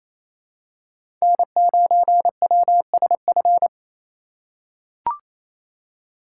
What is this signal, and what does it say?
Morse code, heard as a keyed tone, about 28 wpm, 700 Hz. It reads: N9WSF
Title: USA Call Signs - 5 Characters - 2x Character Spacing - 28wpm